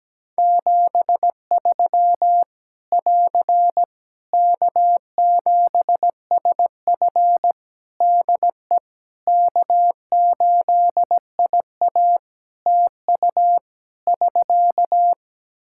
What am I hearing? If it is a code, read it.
Morse code, heard as a keyed tone, about 17 wpm, 700 Hz. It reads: 73 <AR> K7SF DE K8IA TU <SK>
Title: Practice QSO - Exchange 4 - Ending - 17wpm